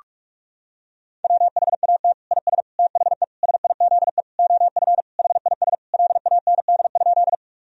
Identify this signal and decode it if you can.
Morse code, heard as a keyed tone, about 41 wpm, 700 Hz. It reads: WHAT IS THE SIZE OF HIS LAND?
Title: Sentences from Top 1000 Words - Review Part 2 - 45wpm